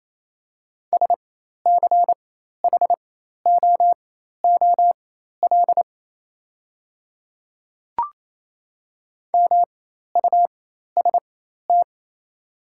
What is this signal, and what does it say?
Morse code, heard as a keyed tone, about 28 wpm, spaced as for 13 wpm, 690 Hz. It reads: SCHOOL MUST
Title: 4x Character Spacing - Top 300 Words - 28wpm